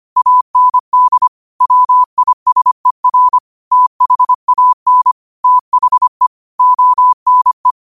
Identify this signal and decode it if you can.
Morse code, heard as a keyed tone, about 25 wpm, 1000 Hz. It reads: AND WISER THAN THE ONE
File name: qotdfiles / 25wpm